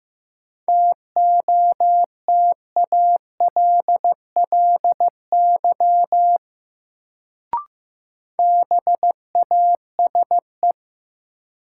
Morse code, heard as a keyed tone, about 15 wpm, 700 Hz. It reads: TOTALLY BASE
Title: Sentences from Top 1600 Words - New Word Drill Alt - 15wpm